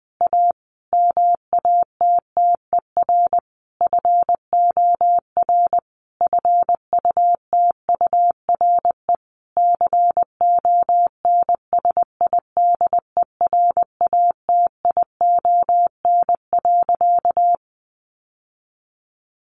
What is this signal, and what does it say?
Morse code, heard as a keyed tone, about 20 wpm, 700 Hz. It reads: A MATTER FOR FUTURE CONSIDERATION.
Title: Sentences from Top 1500 Words 20wpm